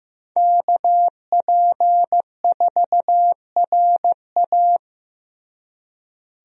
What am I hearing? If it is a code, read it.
Morse code, heard as a keyed tone, about 15 wpm, 700 Hz. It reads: KP4RA
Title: DXCC-100 Call Signs - 5 Characters - Repeated 3x - 15wpm